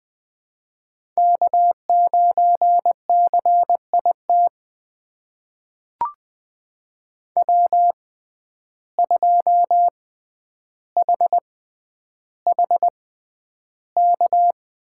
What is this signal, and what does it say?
Morse code, heard as a keyed tone, about 20 wpm, spaced as for 10 wpm, 700 Hz. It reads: K9CIT W2HHK
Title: USA Call Signs - 6x Character Spacing - 20wpm